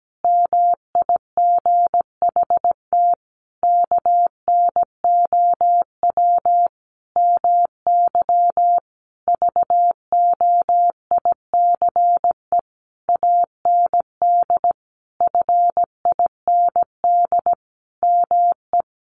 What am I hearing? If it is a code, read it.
Morse code, heard as a keyed tone, about 17 wpm, 700 Hz. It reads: MIGHT KNOW MY VOICE AND FIND ME
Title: Sentences from Top 500 Words 17wpm